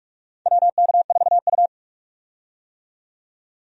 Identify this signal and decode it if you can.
Morse code, heard as a keyed tone, about 44 wpm, 700 Hz. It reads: WK4U
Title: USA Call Signs - 4 Characters - 45wpm